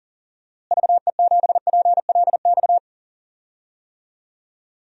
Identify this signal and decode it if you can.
Morse code, heard as a keyed tone, about 40 wpm, 700 Hz. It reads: VE7PLX
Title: DXCC-100 Call Signs - 6 Characters - Repeated 2x - 40wpm